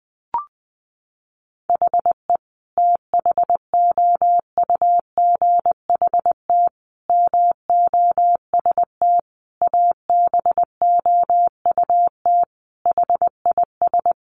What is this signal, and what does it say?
Morse code, heard as a keyed tone, about 20 wpm, 700 Hz. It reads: HE THOUGHT MOST ABOUT HIS
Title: Sentences from Top 1500 Words 20wpm